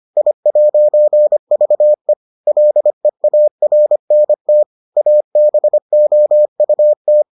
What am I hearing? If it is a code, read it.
Morse code, heard as a keyed tone, about 25 wpm, 600 Hz. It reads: I'VE LEARNT ABOUT